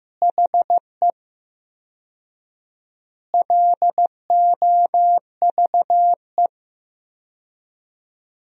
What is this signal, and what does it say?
Morse code, heard as a keyed tone, about 15 wpm, 700 Hz. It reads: HE LOVE
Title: 4x Word-Spacing - Sentences from Top 400 Words - 15wpm